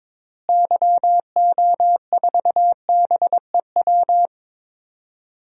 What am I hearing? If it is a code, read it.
Morse code, heard as a keyed tone, about 22 wpm, 700 Hz. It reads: YO4BEW
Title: DXCC-100 Call Signs - 6 Characters - Repeated 3x - 22wpm